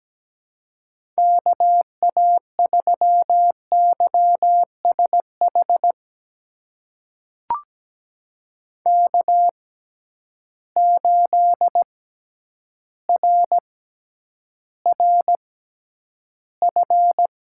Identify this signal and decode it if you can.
Morse code, heard as a keyed tone, about 17 wpm, spaced as for 9 wpm, 700 Hz. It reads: KA3YSH K8RRF
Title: USA Call Signs - 6x Character Spacing - 17wpm